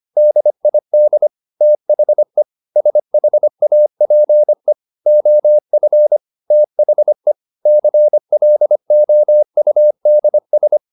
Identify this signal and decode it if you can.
Morse code, heard as a keyed tone, about 25 wpm, 600 Hz. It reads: DID THE SHAPE OF THE CLOUDS
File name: the old man and the sea part 1 25wpm